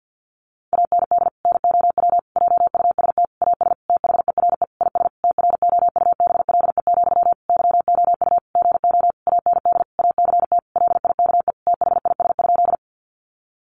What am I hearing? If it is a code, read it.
Morse code, heard as a keyed tone, about 48 wpm, 700 Hz. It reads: AND NOW JUST AS THERE IS TROUBLE, YOU GO AND ACT LIKE THIS?
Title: Sentences from Top 1200 Words 50wpm